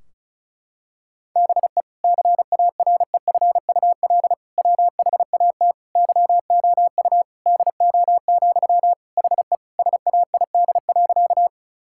Morse code, heard as a keyed tone, about 36 wpm, 700 Hz. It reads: BE CAREFUL WHAT YOU DO, HE SAID.
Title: Sentences from Top 1600 Words 35wpm